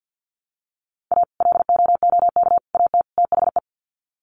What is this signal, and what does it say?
Morse code, heard as a keyed tone, about 50 wpm, 700 Hz. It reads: A LOOK AT THE
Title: Sets of 4 Words 50wpm